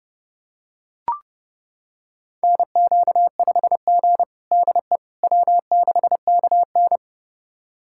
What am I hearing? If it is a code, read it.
Morse code, heard as a keyed tone, about 30 wpm, 700 Hz. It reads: NQ5G DE W6KN
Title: USA Callsign DE Callsign - 4 Characters - 30wpm